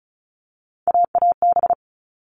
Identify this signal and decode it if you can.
Morse code, heard as a keyed tone, about 34 wpm, 700 Hz. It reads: AAB
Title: All Cut Numbers - 3-Digit 35wpm